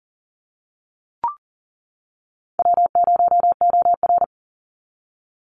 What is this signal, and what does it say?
Morse code, heard as a keyed tone, about 40 wpm, 700 Hz. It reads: W0OR
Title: Easy Call Signs - 40wpm